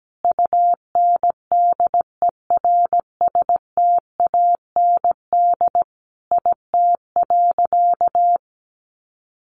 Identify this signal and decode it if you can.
Morse code, heard as a keyed tone, about 17 wpm, 700 Hz. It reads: UNDERSTAND IT.
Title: Sentences from Top 600 Words 17wpm